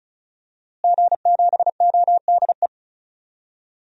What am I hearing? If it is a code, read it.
Morse code, heard as a keyed tone, about 34 wpm, 700 Hz. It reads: G7ODE
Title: DXCC-100 Call Signs - 5 Characters - Repeated 2x - 35wpm